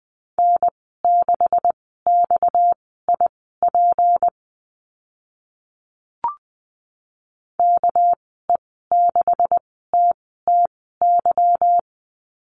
Morse code, heard as a keyed tone, about 20 wpm, spaced as for 14 wpm, 700 Hz. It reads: N6XIP KE6TTY
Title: 231 Lesson 37g - Letter x - Call Signs 20wpm